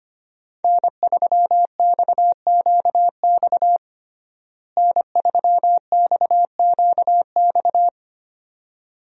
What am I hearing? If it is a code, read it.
Morse code, heard as a keyed tone, about 25 wpm, 700 Hz. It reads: N3XQX N3XQX
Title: USA Call Signs - 5 Characters - Repeated 2x - 25wpm